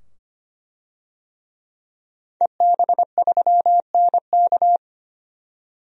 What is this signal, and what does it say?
Morse code, heard as a keyed tone, about 25 wpm, 690 Hz. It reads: EB3NK
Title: DXCC-100 Call Signs - 5 Characters - 25wpm